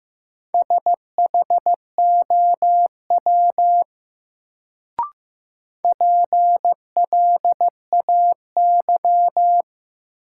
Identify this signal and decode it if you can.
Morse code, heard as a keyed tone, about 15 wpm, 700 Hz. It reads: SHOW PLAY
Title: Sentences from Top 200 Words - New Word Drill - 15wpm